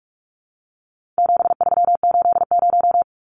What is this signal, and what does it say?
Morse code, heard as a keyed tone, about 44 wpm, 700 Hz. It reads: 7380
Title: Numbers - 4-Digit 45wpm